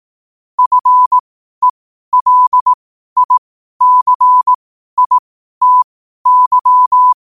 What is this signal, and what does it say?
Morse code, heard as a keyed tone, about 18 wpm, spaced as for 12 wpm, 1000 Hz. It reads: FELICITY
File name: qotdfiles / 12wpm